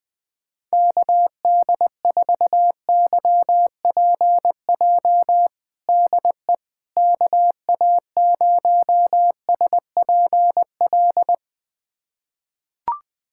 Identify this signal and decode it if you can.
Morse code, heard as a keyed tone, about 21 wpm, 700 Hz. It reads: KD4YPJ DE KA0SPL
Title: USA Callsign DE Callsign - 6 Characters - 20wpm